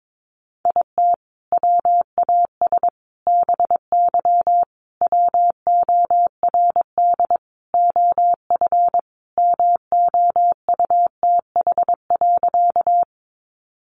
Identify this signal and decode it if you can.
Morse code, heard as a keyed tone, about 22 wpm, 700 Hz. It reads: IT WAS BY WORD OF MOUTH.
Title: Sentences from Top 1500 Words 22wpm